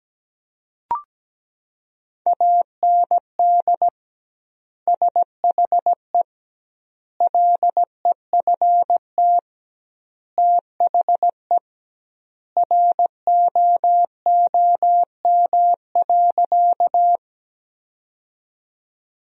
Morse code, heard as a keyed tone, about 17 wpm, 700 Hz. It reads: AND SHE LEFT THE ROOM.
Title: 2x Word-Spacing - Sentences from Top 300 Words - 17wpm